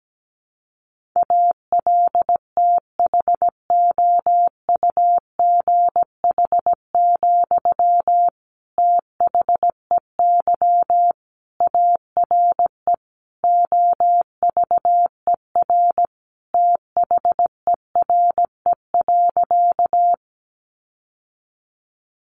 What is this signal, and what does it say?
Morse code, heard as a keyed tone, about 17 wpm, 700 Hz. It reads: ALTHOUGH, THEY ARE OVER THERE.
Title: Sentences from Top 600 Words 17wpm